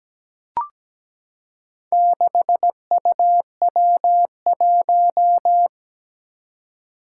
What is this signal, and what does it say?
Morse code, heard as a keyed tone, about 17 wpm, 700 Hz. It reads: 6UW1
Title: Four Challenging Letter-Number Combinations 17wpm